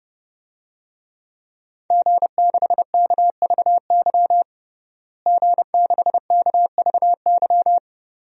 Morse code, heard as a keyed tone, about 32 wpm, 700 Hz. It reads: G6KVY G6KVY
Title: DXCC-100 Call Signs - 5 Characters - Repeated 2x - 30wpm